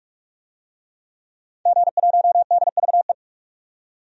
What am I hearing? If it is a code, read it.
Morse code, heard as a keyed tone, about 46 wpm, 700 Hz. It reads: G1DVE
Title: DXCC-100 Call Signs - 5 Characters - 45wpm